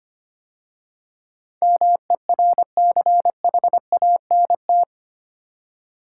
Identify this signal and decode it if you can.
Morse code, heard as a keyed tone, about 25 wpm, 700 Hz. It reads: MERCHANT
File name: 8 Letter Words - First 3 Letters Given - 25wpm